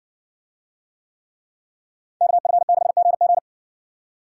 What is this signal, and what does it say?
Morse code, heard as a keyed tone, about 54 wpm, 700 Hz. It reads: KF6GG